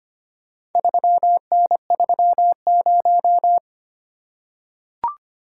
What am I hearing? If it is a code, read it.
Morse code, heard as a keyed tone, about 25 wpm, 700 Hz. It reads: 3N30
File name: practice / 25wpm / Four Letter-Number Combinations - 25wpm